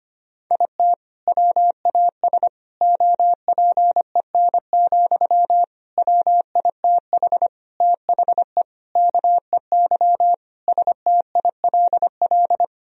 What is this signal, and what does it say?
Morse code, heard as a keyed tone, about 25 wpm, 700 Hz. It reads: IT WAS OPEN, WITH THE KEY STILL
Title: Sentences from Top 1000 Words - Review Part 4 - 25wpm